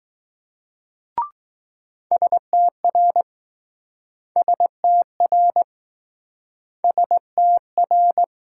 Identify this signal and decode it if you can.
Morse code, heard as a keyed tone, about 20 wpm, 700 Hz. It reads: STR STR STR